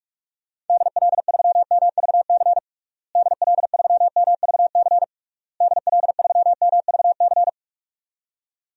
Morse code, heard as a keyed tone, about 46 wpm, 700 Hz. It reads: DL3MVC DL3MVC DL3MVC
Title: DXCC-100 Call Signs - 6 Characters - Repeated 3x - 45wpm